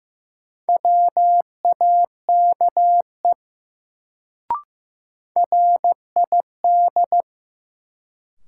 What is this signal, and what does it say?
Morse code, heard as a keyed tone, about 15 wpm, 700 Hz. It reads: WAKE RID
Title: Sets of 4 Words - New Word Drill - 15wpm